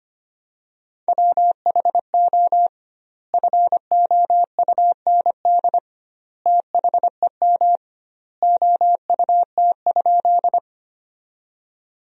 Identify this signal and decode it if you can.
Morse code, heard as a keyed tone, about 25 wpm, 700 Hz. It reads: WHO FOUND THEM OUT?
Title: 2x Word-Spacing - Sentences from Top 300 Words - 25wpm